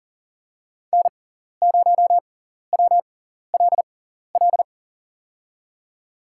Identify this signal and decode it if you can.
Morse code, heard as a keyed tone, about 37 wpm, spaced as for 14 wpm, 700 Hz. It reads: N0WLL
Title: USA Call Signs - 5 Characters - 6x Character Spacing - 40wpm